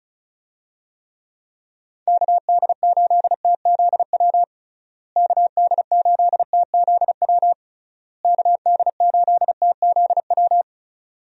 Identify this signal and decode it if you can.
Morse code, heard as a keyed tone, about 34 wpm, 700 Hz. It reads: KD8TZW KD8TZW KD8TZW
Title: USA Call Signs - 6 Characters - Repeated 3x - 35wpm